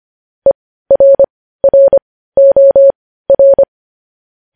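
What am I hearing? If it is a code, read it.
Morse code, heard as a keyed tone, about 25 wpm, spaced as for 15 wpm, 550 Hz. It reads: ERROR
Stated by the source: Morse code news headlines sent at 15 WPM.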